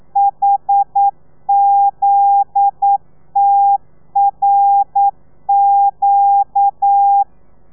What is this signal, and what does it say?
Morse code, heard as a keyed tone, about 9 wpm, 800 Hz. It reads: HZTRQ
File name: telegrafia / i2tfj / Lez9 40